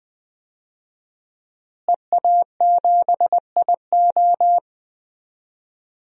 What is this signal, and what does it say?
Morse code, heard as a keyed tone, about 20 wpm, 700 Hz. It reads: EA7IO
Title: DXCC-100 Call Signs - 5 Characters - Repeated 3x - 20wpm